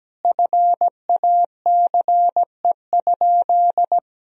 Morse code, heard as a keyed tone, about 17 wpm, 700 Hz. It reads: FACE?
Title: Sentences from Top 1000 Words - Review Part 2 - 17wpm